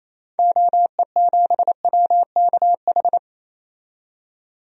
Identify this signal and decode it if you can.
Morse code, heard as a keyed tone, about 28 wpm, 700 Hz. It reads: OE7WKH
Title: DXCC-100 Call Signs - 6 Characters - 28wpm